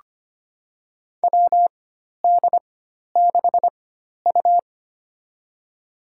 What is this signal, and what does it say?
Morse code, heard as a keyed tone, about 25 wpm, spaced as for 12 wpm, 700 Hz. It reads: WD6U
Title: USA Call Signs - 4 Characters - 4x Character Spacing - 25wpm